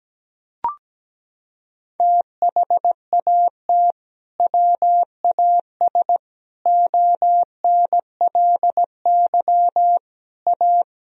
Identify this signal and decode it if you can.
Morse code, heard as a keyed tone, about 17 wpm, 700 Hz. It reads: THAT WAS ONLY A
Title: Sentences from Top 1000 Words - Review Part 1 - 17wpm